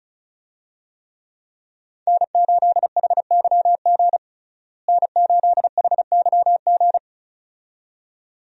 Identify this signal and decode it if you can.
Morse code, heard as a keyed tone, about 34 wpm, 700 Hz. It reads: N8HYG N8HYG
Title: USA Call Signs - 5 Characters - Repeated 2x - 35wpm